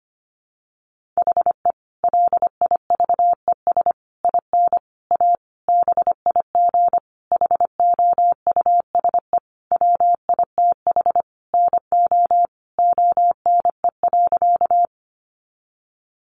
Morse code, heard as a keyed tone, about 25 wpm, 700 Hz. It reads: HE LIVES IN A BIG HOUSE WITH NO ONE.
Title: Additional Easy Phrases 25wpm